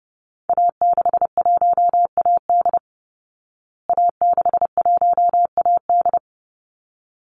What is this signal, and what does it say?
Morse code, heard as a keyed tone, about 30 wpm, 700 Hz. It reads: A61AD A61AD
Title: Call Signs Repeated 3x - 30wpm